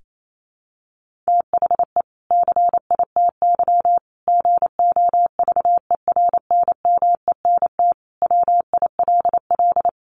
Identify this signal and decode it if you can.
Morse code, heard as a keyed tone, about 28 wpm, 700 Hz. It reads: THE CITY GOVERNMENT WILL
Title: Sentences from Top 1500 Words 28wpm